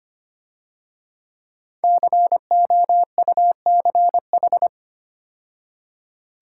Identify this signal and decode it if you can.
Morse code, heard as a keyed tone, about 30 wpm, 700 Hz. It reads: COUCH
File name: Easy Words - Speed-Racing with Hints - 25wpm